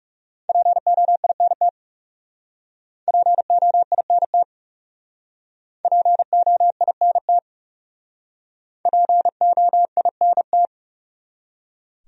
Morse code, heard as a keyed tone, about 37 wpm, 700 Hz. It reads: POINT POINT POINT POINT
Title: Speed-Racing Top 500 Words 30wpm